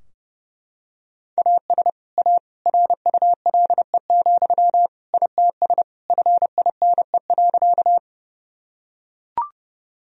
Morse code, heard as a keyed tone, about 30 wpm, 700 Hz. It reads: AS A RULE, ITS FINE.
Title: Sentences from Top 1000 Words - Review Part 2 - 30wpm